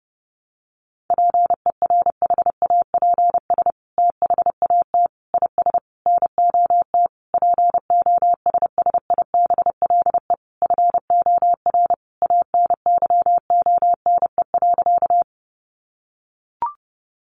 Morse code, heard as a keyed tone, about 30 wpm, 700 Hz. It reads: PERHAPS THAT IS NOT POSSIBLE FOR ANYONE.
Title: Sentences from Top 900 Words 30wpm